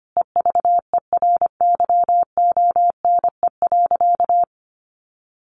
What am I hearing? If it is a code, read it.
Morse code, heard as a keyed tone, about 25 wpm, 700 Hz. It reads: EVERYONE.
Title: A Sound Of Thunder 25wpm